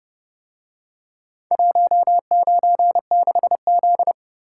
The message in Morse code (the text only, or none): 196Z